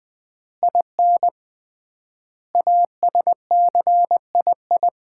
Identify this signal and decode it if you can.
Morse code, heard as a keyed tone, about 20 wpm, 690 Hz. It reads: IN ASCII
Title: 3x Word-Spacing - Sets of 5 Words - 20wpm